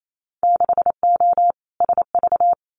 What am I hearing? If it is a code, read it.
Morse code, heard as a keyed tone, about 28 wpm, 700 Hz. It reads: 6O SV
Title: Winter Field Day Exchanges Encore 28wpm